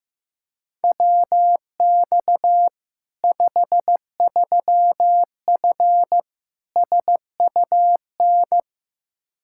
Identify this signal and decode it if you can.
Morse code, heard as a keyed tone, about 15 wpm, 700 Hz. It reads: WX 53F SUN
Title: Practice QSO - Exchange 2 - Info - 15wpm